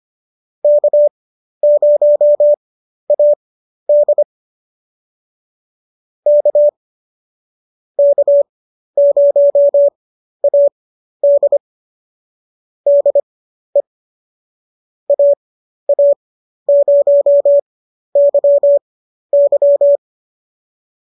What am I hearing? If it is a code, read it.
Morse code, heard as a keyed tone, about 25 wpm, spaced as for 12 wpm, 590 Hz. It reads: K0AD K K0AD DE AA0YY